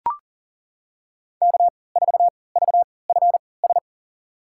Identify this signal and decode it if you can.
Morse code, heard as a keyed tone, about 38 wpm, spaced as for 22 wpm, 700 Hz. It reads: K4VFS
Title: USA Call Signs - 5 Characters - 3x Character Spacing - 40wpm